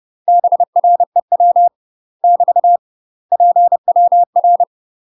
Morse code, heard as a keyed tone, about 30 wpm, 700 Hz. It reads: BREW <BT> PWR
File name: UK MASTER QSOs TRAINING / 30WPM / UK MASTER QSOs 0028